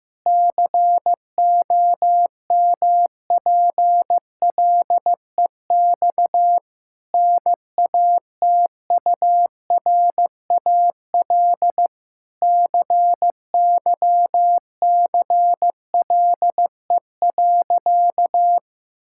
Morse code, heard as a keyed tone, about 15 wpm, 700 Hz. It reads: COMPLEX NATURAL CYCLE.
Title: Sentences from Top 1000 Words - Review Part 6 - 15wpm